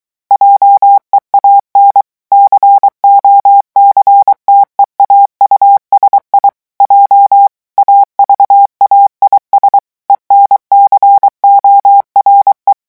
Morse code, heard as a keyed tone, about 23 wpm, 800 Hz. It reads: JEAN COCTEAU SI J AVAIS ENCORE
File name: morse mp3 / revision9 citations humoristiques vitesse 22 mots minute